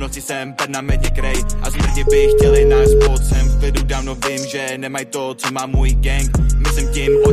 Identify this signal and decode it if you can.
Ringback tone (Europe, 425 Hz), with noise.